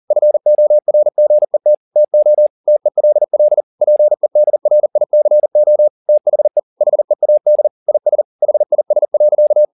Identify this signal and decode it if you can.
Morse code, heard as a keyed tone, about 40 wpm, 600 Hz. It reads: FORGET TO TELL PEDRICO THE HEAD IS HIS.
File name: the old man and the sea 2 40wpm